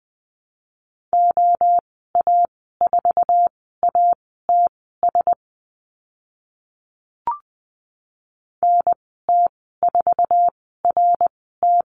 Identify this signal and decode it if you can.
Morse code, heard as a keyed tone, about 20 wpm, spaced as for 14 wpm, 700 Hz. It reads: OA4ATS NT4RT
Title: 062 Lesson 11g - Letter h - Call Signs 20wpm